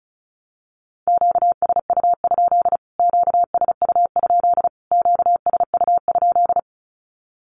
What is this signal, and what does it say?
Morse code, heard as a keyed tone, about 34 wpm, 700 Hz. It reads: QSU? QSU? QSU?
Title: Q-Codes Repeated 3x 35wpm